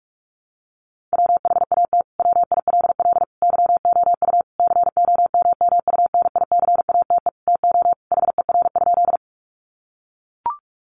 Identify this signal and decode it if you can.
Morse code, heard as a keyed tone, about 44 wpm, 700 Hz. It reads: WHAT WILL YOU COMMUNICATE TO HER?
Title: Sentences from Top 1300 Words 45wpm